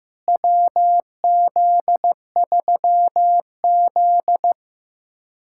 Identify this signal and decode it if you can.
Morse code, heard as a keyed tone, about 15 wpm, 700 Hz. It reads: WZ3Z
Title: USA Callsign DE Callsign - 4 Characters - 15wpm